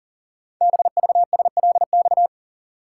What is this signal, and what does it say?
Morse code, heard as a keyed tone, about 43 wpm, 700 Hz. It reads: BVSLX
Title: Code Groups - 5 Letter:Numbers - 40wpm